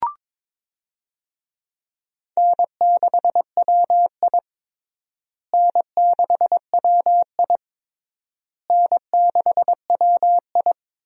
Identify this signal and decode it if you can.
Morse code, heard as a keyed tone, about 22 wpm, 700 Hz. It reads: N6WI N6WI N6WI
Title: USA Call Signs - 4 Characters - Repeated 3x - 22wpm